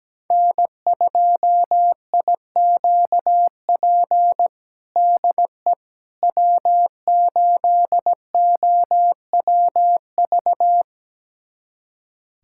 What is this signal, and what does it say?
Morse code, heard as a keyed tone, about 17 wpm, 700 Hz. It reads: N2IQP DE W8OWV
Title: USA Callsign DE Callsign - 5 Characters - 17wpm